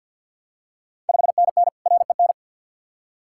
Morse code, heard as a keyed tone, about 48 wpm, 690 Hz. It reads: 5NN REN
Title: Ontario QSO Party Exchange 50wpm